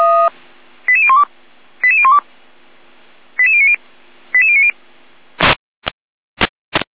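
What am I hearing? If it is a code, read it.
Five-tone selective call (ZVEI), heard as quick runs of five tones, tones of about 70 ms. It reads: 88012 88012 88090 88090
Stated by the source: ZVEI five-tone selective call 88399 received on SDR.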